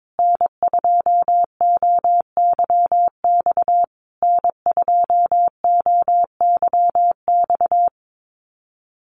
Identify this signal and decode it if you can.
Morse code, heard as a keyed tone, about 22 wpm, 700 Hz. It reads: N2OYX N2OYX
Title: USA Call Signs Repeated 3x 22wpm